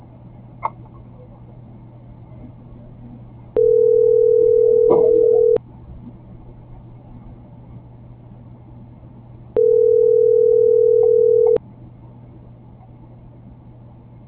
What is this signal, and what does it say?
Ringback tone (North America), 440 + 480 Hz.